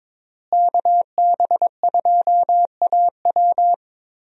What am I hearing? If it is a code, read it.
Morse code, heard as a keyed tone, about 22 wpm, 700 Hz. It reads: KB2AW
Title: USA Call Signs - 5 Characters - Repeated 3x - 22wpm